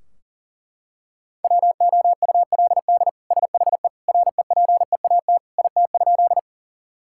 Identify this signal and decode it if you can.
Morse code, heard as a keyed tone, about 40 wpm, 700 Hz. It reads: WOULD SHE REPEAT IT?
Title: Sentences from Top 1600 Words 40wpm